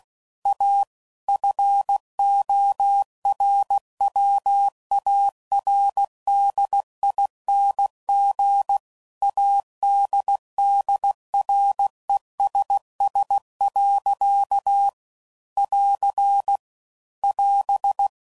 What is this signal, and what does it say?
Morse code, heard as a keyed tone, about 16 wpm, 800 Hz. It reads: A FORWARDING ADDRESS. <AR> <AS>